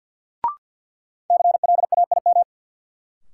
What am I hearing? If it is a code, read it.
Morse code, heard as a keyed tone, about 48 wpm, 700 Hz. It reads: CLAIM
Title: Sentences from Top 1500 Words - New Word Drill - 50wpm